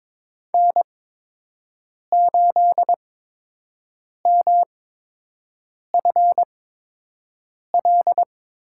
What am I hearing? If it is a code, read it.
Morse code, heard as a keyed tone, about 22 wpm, spaced as for 6 wpm, 700 Hz. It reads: N8MFL
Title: USA Call Signs - 5 Characters - 8x Character Spacing - 22wpm